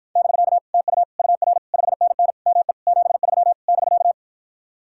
Morse code, heard as a keyed tone, about 53 wpm, 700 Hz. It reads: <BK> TU UR 5NN ME 73 <BK>